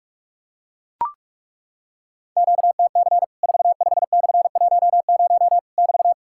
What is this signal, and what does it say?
Morse code, heard as a keyed tone, about 43 wpm, 700 Hz. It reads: QTC 45/10 <BT>
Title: Worked All Europe DX Contest - QTC Exchange - 45wpm